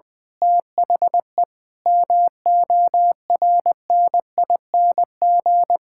Morse code, heard as a keyed tone, about 20 wpm, 700 Hz. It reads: THE MORNING
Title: Pangram Challenge v2 - 20wpm